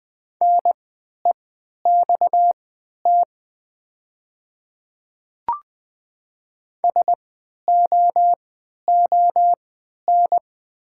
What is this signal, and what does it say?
Morse code, heard as a keyed tone, about 20 wpm, spaced as for 11 wpm, 700 Hz. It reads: NEXT SOON
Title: 3x Character Spacing - Top 500 Words - 20wpm